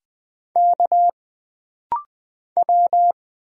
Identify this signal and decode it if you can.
Morse code, heard as a keyed tone, about 20 wpm, 700 Hz. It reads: K W
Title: Single Letter-Number - Rapid-Fire 20wpm